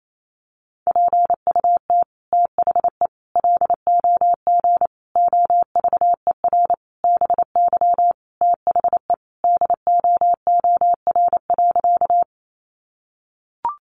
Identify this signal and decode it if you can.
Morse code, heard as a keyed tone, about 28 wpm, 700 Hz. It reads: PUT THE LOG OVER BY THE DOOR.
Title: Sentences from Top 1400 Words 28wpm